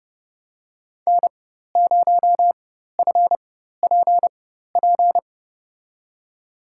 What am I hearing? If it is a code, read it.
Morse code, heard as a keyed tone, about 29 wpm, spaced as for 14 wpm, 700 Hz. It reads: N0FPP
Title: USA Call Signs - 5 Characters - 4x Character Spacing - 30wpm